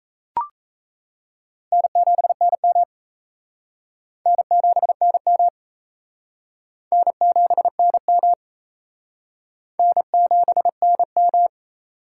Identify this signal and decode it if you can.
Morse code, heard as a keyed tone, about 37 wpm, 700 Hz. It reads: N7NM N7NM N7NM N7NM